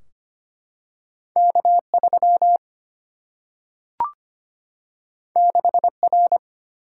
Morse code, heard as a keyed tone, about 25 wpm, 700 Hz. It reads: K3 6R
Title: Single-Letter Single-Number 25wpm